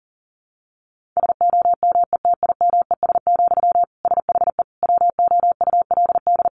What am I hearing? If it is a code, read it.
Morse code, heard as a keyed tone, about 40 wpm, 700 Hz. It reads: SOMETIMES, SHE WOULD
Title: Sentences from Top 1000 Words 40wpm